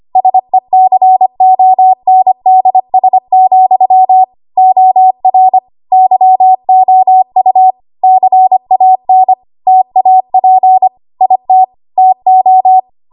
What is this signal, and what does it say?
Morse code, heard as a keyed tone, about 25 wpm, 750 Hz. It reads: SECONDS, OR YOU CAN TAP IT TO
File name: Archive / 25 WPM / 25WPM